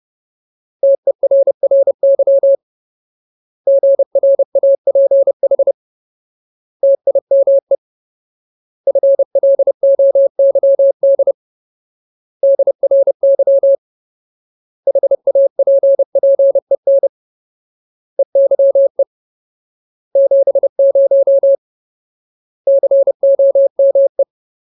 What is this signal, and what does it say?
Morse code, heard as a keyed tone, about 30 wpm, 550 Hz. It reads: TERRY GRAPH TIME FLOYD DRY HAPPEN EYE 70 COME
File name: Ultimate Word List / 30WPM Ultimate